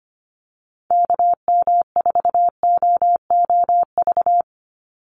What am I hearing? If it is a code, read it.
Morse code, heard as a keyed tone, about 25 wpm, 700 Hz. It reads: KM4OOV
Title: USA Call Signs 25wpm